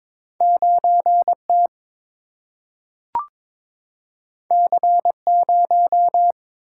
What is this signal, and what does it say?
Morse code, heard as a keyed tone, about 22 wpm, 700 Hz. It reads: 9 T C 0
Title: Single-Letter Single-Number 22wpm